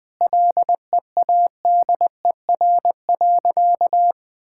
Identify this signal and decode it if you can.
Morse code, heard as a keyed tone, about 20 wpm, 700 Hz. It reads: LEADER.
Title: Sentences with 3-Word Phrases - 20wpm